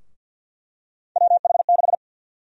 Morse code, heard as a keyed tone, about 52 wpm, 700 Hz. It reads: WH6
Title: USA Call Sign Prefixes - Repeated 2x - 50wpm